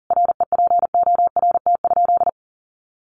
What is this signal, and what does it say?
Morse code, heard as a keyed tone, about 40 wpm, 700 Hz. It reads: REPORT?
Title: Sentences from Top 1200 Words 40wpm